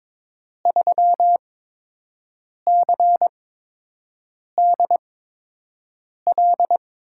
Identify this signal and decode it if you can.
Morse code, heard as a keyed tone, about 22 wpm, spaced as for 6 wpm, 700 Hz. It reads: 3CDL
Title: USA Call Signs - 8x Character Spacing - 22wpm